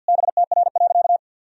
Morse code, heard as a keyed tone, about 52 wpm, 700 Hz. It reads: BTR.
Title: Sentences from Top 500 Words and Abbreviations 50wpm